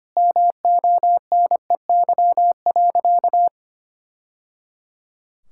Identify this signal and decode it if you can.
Morse code, heard as a keyed tone, about 25 wpm, 700 Hz. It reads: MONEY.
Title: Sentences from Top 1000 Words - Review Part 6 - 25wpm